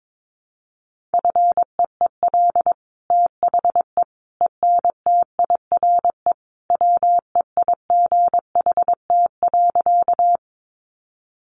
Morse code, heard as a keyed tone, about 22 wpm, 700 Hz. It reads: FEEL THE ENTIRE WEIGHT.
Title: Sentences from Top 800 Words 22wpm